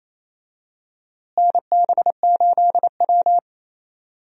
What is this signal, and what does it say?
Morse code, heard as a keyed tone, about 28 wpm, 700 Hz. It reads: NB8W
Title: USA Call Signs - 4 Characters - Repeated 2x - 28wpm